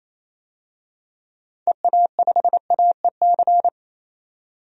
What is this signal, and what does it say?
Morse code, heard as a keyed tone, about 27 wpm, 700 Hz. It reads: EA5AEC
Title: DXCC-100 Call Signs - 6 Characters - 28wpm